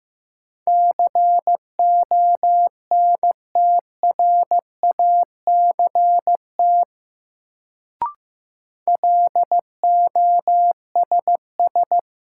Morse code, heard as a keyed tone, about 16 wpm, 700 Hz. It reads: CONTRACT LOSS
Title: Sentences from Top 1400 Words - New Word Drill Alt - 15wpm